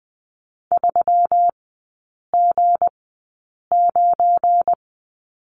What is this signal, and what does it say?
Morse code, heard as a keyed tone, about 20 wpm, spaced as for 8 wpm, 700 Hz. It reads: 3G9
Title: 172 Lesson 28e - Number 9 - Three Characters 20wpm